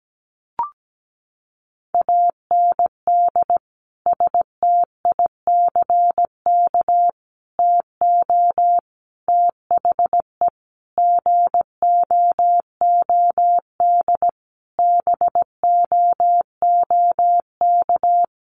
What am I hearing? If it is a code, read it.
Morse code, heard as a keyed tone, about 17 wpm, 700 Hz. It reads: AND STICK TO THE GOOD BOOK
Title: Sentences from Top 1200 Words 17wpm